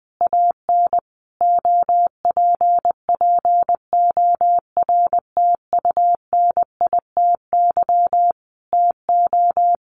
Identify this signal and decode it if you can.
Morse code, heard as a keyed tone, about 20 wpm, 700 Hz. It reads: AN OPPORTUNITY TO
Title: Sentences from Top 1000 Words 20wpm